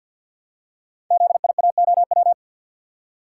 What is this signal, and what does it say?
Morse code, heard as a keyed tone, about 52 wpm, 700 Hz. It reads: ZIAOW
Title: Code Groups - 5 Letter:Numbers - 50wpm